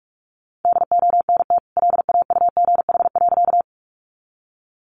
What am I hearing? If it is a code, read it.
Morse code, heard as a keyed tone, about 44 wpm, 700 Hz. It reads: DONT LAUGH.
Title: Sentences from Top 500 Words 45wpm